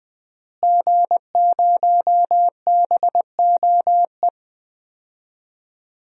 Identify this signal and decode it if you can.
Morse code, heard as a keyed tone, about 21 wpm, 700 Hz. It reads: G0BOE
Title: DXCC-100 Call Signs - 5 Characters - Repeated 2x - 20wpm